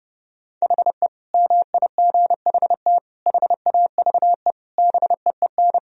Morse code, heard as a keyed tone, about 32 wpm, 700 Hz. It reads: HE MIGHT HAVE BEEN
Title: Sentences from Top 1000 Words - Review Part 2 - 30wpm